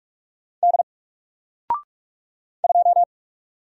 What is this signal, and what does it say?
Morse code, heard as a keyed tone, about 44 wpm, 700 Hz. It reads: D 2
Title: Single Letter-Number Alt 45wpm